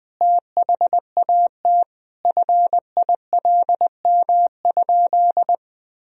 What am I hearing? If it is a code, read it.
Morse code, heard as a keyed tone, about 20 wpm, 700 Hz. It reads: THAT FILM?
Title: Sentences from Top 1000 Words - Review Part 3 - 20wpm